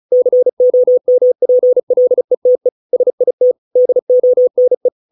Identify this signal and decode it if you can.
Morse code, heard as a keyed tone, about 34 wpm, 500 Hz. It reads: COMPLETE SIT DONE